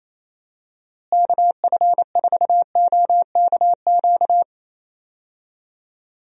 Morse code, heard as a keyed tone, about 27 wpm, 700 Hz. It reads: KF4OKQ
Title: USA Call Signs - 6 Characters - Repeated 3x - 28wpm